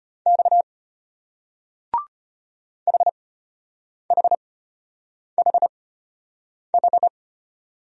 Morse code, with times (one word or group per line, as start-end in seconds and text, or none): 0.3-0.6 X
2.9-7.1 HHHH